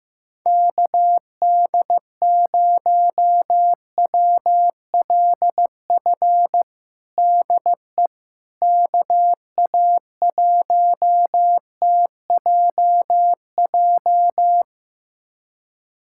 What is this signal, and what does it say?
Morse code, heard as a keyed tone, about 15 wpm, 700 Hz. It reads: KD0WLF DE KA1TJJ
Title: USA Callsign DE Callsign - 6 Characters - 15wpm